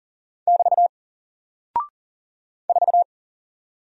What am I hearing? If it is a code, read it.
Morse code, heard as a keyed tone, about 40 wpm, 700 Hz. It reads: <BT> 4
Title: Single Character (Letter:Number:Punctuation:Prosign) 40wpm